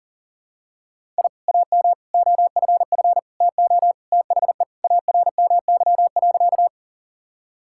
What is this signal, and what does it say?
Morse code, heard as a keyed tone, about 40 wpm, 700 Hz. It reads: I AM OFF TO THE ARMY.
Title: Sentences from Top 1000 Words - Review Part 5 - 40wpm